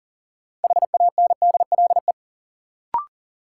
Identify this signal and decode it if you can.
Morse code, heard as a keyed tone, about 40 wpm, 700 Hz. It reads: HANDLE
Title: Sentences from Top 1100 Words - New Word Drill - 40wpm